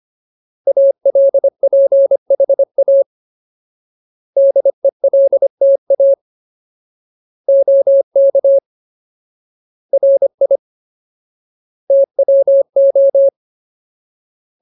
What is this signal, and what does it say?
Morse code, heard as a keyed tone, about 25 wpm, 550 Hz. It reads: ALPHA DELTA OK RI TWO
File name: Ultimate Word List / 25WPM Ultimate